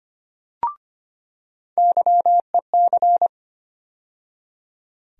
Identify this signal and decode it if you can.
Morse code, heard as a keyed tone, about 25 wpm, 700 Hz. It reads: YEC
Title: USA Call Sign Postfixes - 3 Letters 25wpm